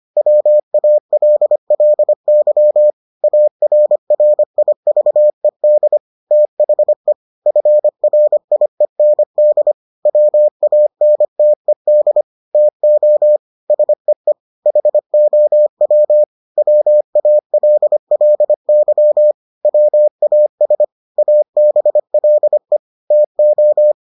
Morse code, heard as a keyed tone, about 25 wpm, 600 Hz. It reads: WALLY ARRIVED THE FRIEND WANTED TO SEE HOW WALLY WAS ABLE TO